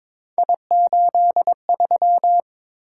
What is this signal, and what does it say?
Morse code, heard as a keyed tone, about 22 wpm, 700 Hz. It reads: I83
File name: practice / 22wpm / Three Letter-Number Combinations - 22wpm